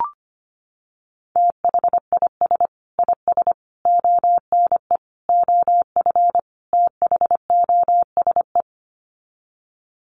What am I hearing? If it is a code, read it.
Morse code, heard as a keyed tone, about 25 wpm, 700 Hz. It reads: THIS IS ONE OF THOSE
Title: Sets of 5 Words Encore2 25wpm